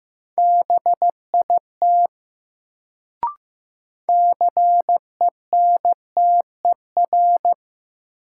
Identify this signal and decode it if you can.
Morse code, heard as a keyed tone, about 15 wpm, 700 Hz. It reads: BIT CENTER
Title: Sets of 3 Words - New Word Drill - 15wpm